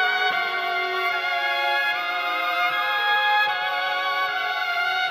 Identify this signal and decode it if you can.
Emergency Alert System (SAME) tones, heard as harsh buzzing bursts. It is heard from 1.1 to 4.7 s.